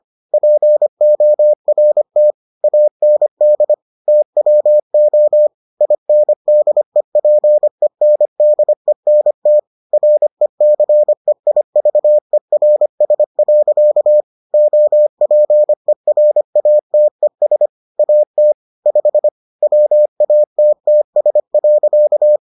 Morse code, heard as a keyed tone, about 25 wpm, 600 Hz. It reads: PORT AND TWO INDEPENDENT RECEIVERS. OPERATES AT 5 WATTS.